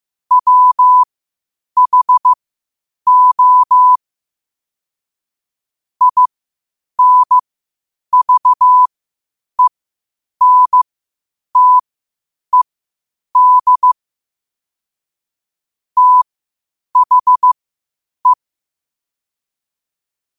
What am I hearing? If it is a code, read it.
Morse code, heard as a keyed tone, about 15 wpm, spaced as for 8 wpm, 1000 Hz. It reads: WHO INVENTED THE
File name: 8wpm